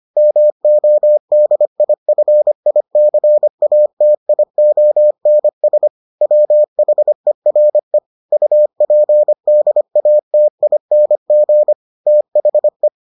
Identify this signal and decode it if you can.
Morse code, heard as a keyed tone, about 25 wpm, 600 Hz. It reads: MODIFICATIONS WHERE UPDATING THE